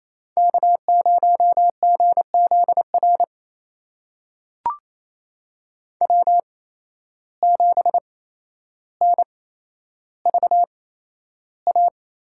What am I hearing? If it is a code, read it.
Morse code, heard as a keyed tone, about 28 wpm, spaced as for 12 wpm, 700 Hz. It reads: K0GZR W7NVA
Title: USA Call Signs - 5 Characters - 8x Character Spacing - 28wpm